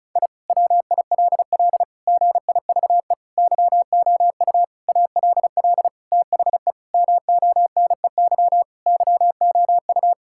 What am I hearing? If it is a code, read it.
Morse code, heard as a keyed tone, about 36 wpm, 700 Hz. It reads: I WILL GIVE YOU ALL THE MONEY YOU
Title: Sentences from Top 1000 Words - Review Part 6 - 35wpm